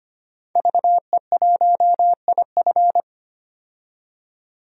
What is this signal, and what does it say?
Morse code, heard as a keyed tone, about 25 wpm, 700 Hz. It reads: VE1IF
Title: DXCC-100 Call Signs - 5 Characters - 25wpm